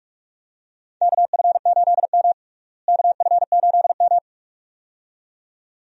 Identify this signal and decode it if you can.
Morse code, heard as a keyed tone, about 44 wpm, 700 Hz. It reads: KF8M KF8M
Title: USA Call Signs - 4 Characters - Repeated 2x - 45wpm